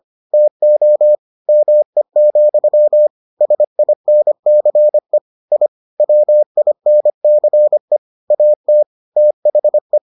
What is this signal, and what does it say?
Morse code, heard as a keyed tone, about 25 wpm, 600 Hz. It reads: TO ME, SINCE I WINCE AT THE